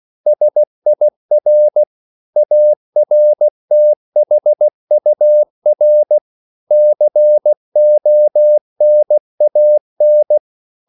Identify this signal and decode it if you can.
Morse code, heard as a keyed tone, about 16 wpm, 600 Hz. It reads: SIR ARTHUR CONAN
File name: CW / Practice / EnglishQuotations0216